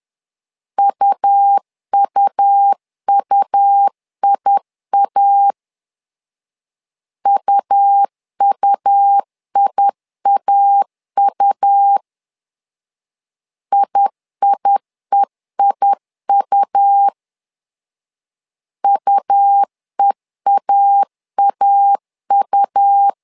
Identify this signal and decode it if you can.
Morse code, heard as a keyed tone, about 10 wpm, 800 Hz. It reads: UUUIA UUIAU IIEIU UEAAU
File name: LEZIONI / 2 UHDN / 2 U 10wpm 1622